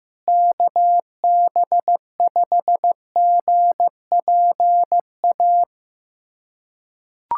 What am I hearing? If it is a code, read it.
Morse code, heard as a keyed tone, about 15 wpm, 700 Hz. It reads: KB5GPA
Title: USA Callsign DE Callsign - 6 Characters - 15wpm